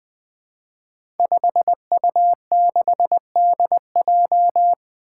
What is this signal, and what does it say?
Morse code, heard as a keyed tone, about 20 wpm, 700 Hz. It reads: 5U6DJ
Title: Five Challenging Letter-Number Combinations 20wpm